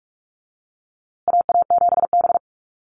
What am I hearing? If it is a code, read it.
Morse code, heard as a keyed tone, about 44 wpm, 690 Hz. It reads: AA7B
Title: USA Call Signs Repeated 2x 45wpm